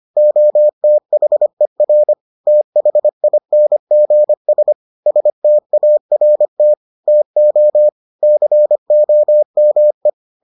Morse code, heard as a keyed tone, about 25 wpm, 600 Hz. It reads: OTHER THINGS START TO COME